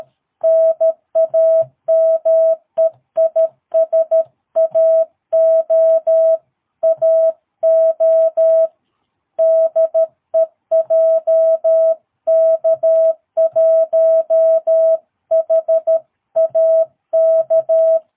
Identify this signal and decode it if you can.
Morse code, heard as a keyed tone, about 13 wpm, 650 Hz. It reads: NAME IS AO AO DE JK1HAK
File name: jh8cbh / 2022cw